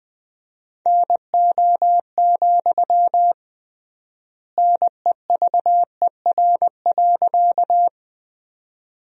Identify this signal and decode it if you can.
Morse code, heard as a keyed tone, about 20 wpm, 700 Hz. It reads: NO, NEVER.
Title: 3x Word-Spacing - Sentences from Top 300 Words - 20wpm